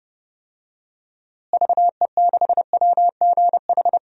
Morse code, heard as a keyed tone, about 31 wpm, 700 Hz. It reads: VE6WGH
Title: DXCC-100 Call Signs - 6 Characters - Repeated 2x - 30wpm